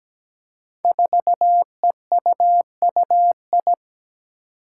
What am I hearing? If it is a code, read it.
Morse code, heard as a keyed tone, about 17 wpm, 700 Hz. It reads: 4EUUI
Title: Five Challenging Letter-Number Combinations 17wpm